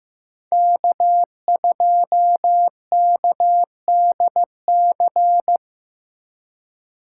Morse code, heard as a keyed tone, about 18 wpm, 700 Hz. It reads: K2KDC
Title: Speed Racing - USA 1x3 Call Signs 15wpm